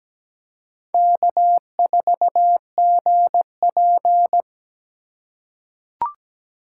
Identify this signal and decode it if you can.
Morse code, heard as a keyed tone, about 17 wpm, 700 Hz. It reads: K4GP
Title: USA Call Signs - 4 Characters - 8x Character Spacing - 17wpm